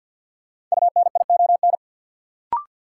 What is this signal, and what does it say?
Morse code, heard as a keyed tone, about 50 wpm, 700 Hz. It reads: UNION
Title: Sentences from Top 1400 Words - New Word Drill Alt - 50wpm